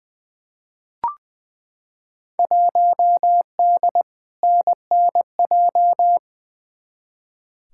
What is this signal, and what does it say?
Morse code, heard as a keyed tone, about 21 wpm, 700 Hz. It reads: 1D NNJ
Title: Field Day Exchanges Encore - 20wpm